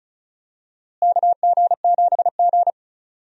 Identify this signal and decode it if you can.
Morse code, heard as a keyed tone, about 34 wpm, 700 Hz. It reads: KG7G
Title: USA Call Signs - 4 Characters - Repeated 2x - 35wpm